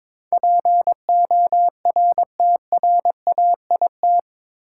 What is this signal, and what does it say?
Morse code, heard as a keyed tone, about 22 wpm, 700 Hz. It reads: PORTRAIT
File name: practice / 22wpm / Eight Letter Words - 22wpm